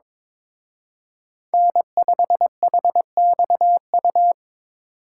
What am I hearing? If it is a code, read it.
Morse code, heard as a keyed tone, about 27 wpm, 700 Hz. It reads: N5HXU
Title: Speed Racing - USA Call Signs 22wpm